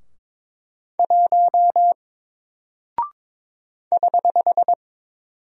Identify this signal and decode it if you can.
Morse code, heard as a keyed tone, about 22 wpm, 700 Hz. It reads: 1 <HH>
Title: Single Character (Letter:Number:Punctuation:Prosign) Alt 22wpm